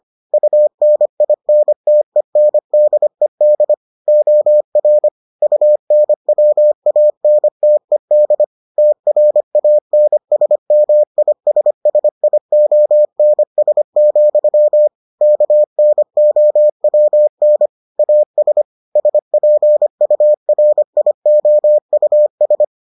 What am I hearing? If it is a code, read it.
Morse code, heard as a keyed tone, about 25 wpm, 600 Hz. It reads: UNINTENDED OR UNWANTED TRANSMISSIONS, KNOWN AS SPURIOUS